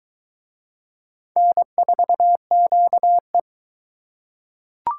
Morse code, heard as a keyed tone, about 21 wpm, 700 Hz. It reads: N4QE